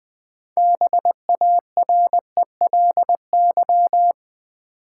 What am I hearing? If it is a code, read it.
Morse code, heard as a keyed tone, about 20 wpm, 700 Hz. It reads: BARELY
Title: 6 Letter Words - First 3 Letters Given - 20wpm